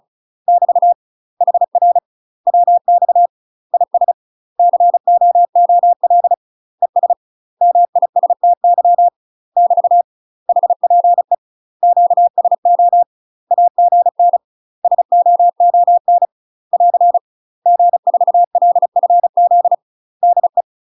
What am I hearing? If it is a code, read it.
Morse code, heard as a keyed tone, about 34 wpm, 700 Hz. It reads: <BT> HR WX IS COOL ES MISTY <BT> HPE QSO AGN SOON <AR> G4LFZ DE
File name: UK MASTER QSOs TRAINING / 35WPM / UK MASTER QSOs 0025